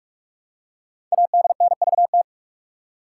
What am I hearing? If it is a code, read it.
Morse code, heard as a keyed tone, about 44 wpm, 700 Hz. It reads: ADNVT
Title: Code Groups - 5 Letters - 45wpm